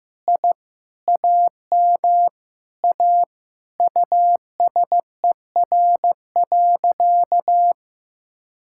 Morse code, heard as a keyed tone, about 15 wpm, 700 Hz. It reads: I AM A USER.
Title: Sentences from Top 1000 Words - Review Part 5 - 15wpm